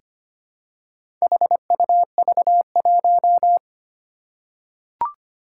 Morse code, heard as a keyed tone, about 25 wpm, 700 Hz. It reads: HUV1
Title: Four Challenging Letter-Number Combinations 25wpm